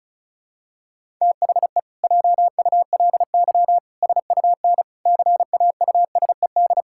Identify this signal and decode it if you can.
Morse code, heard as a keyed tone, about 34 wpm, 700 Hz. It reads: THE JULY SUN CAUSED
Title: Pangram Challenge v2 - 35wpm